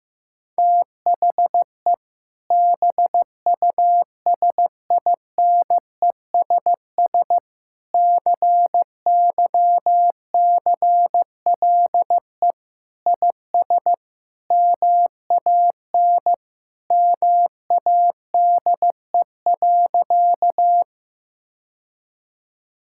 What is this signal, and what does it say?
Morse code, heard as a keyed tone, about 14 wpm, 700 Hz. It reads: THE BUSINESS CYCLE IS MAN MADE.
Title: Sentences from Top 1000 Words - Review Part 6 - 15wpm